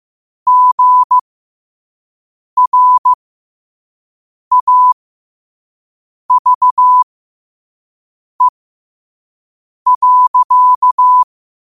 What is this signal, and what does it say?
Morse code, heard as a keyed tone, about 15 wpm, spaced as for 5 wpm, 1000 Hz. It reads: GRAVE.
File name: 5wpm